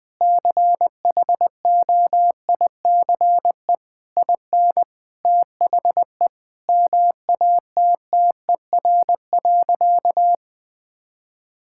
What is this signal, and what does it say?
Morse code, heard as a keyed tone, about 20 wpm, 700 Hz. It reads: CHOICE IN THE MATTER.
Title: Sentences from Top 1000 Words - Review Part 6 - 20wpm